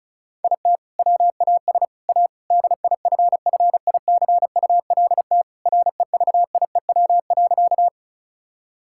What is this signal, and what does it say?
Morse code, heard as a keyed tone, about 36 wpm, 690 Hz. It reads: IT WAS A DIFFICULT REVIEW.
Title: Sentences from Top 1000 Words - Review Part 6 - 35wpm